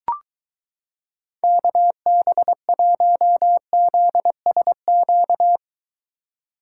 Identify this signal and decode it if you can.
Morse code, heard as a keyed tone, about 23 wpm, 700 Hz. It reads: KB1ZSQ